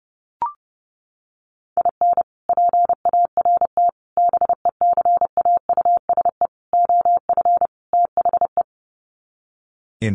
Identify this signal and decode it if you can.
Morse code, heard as a keyed tone, about 30 wpm, 700 Hz. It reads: IN PART BECAUSE OF THE
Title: Sets of 5 Words Encore2 30wpm